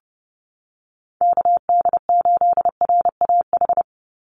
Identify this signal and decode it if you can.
Morse code, heard as a keyed tone, about 30 wpm, 700 Hz. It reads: KD8RAH
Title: USA Call Signs 30wpm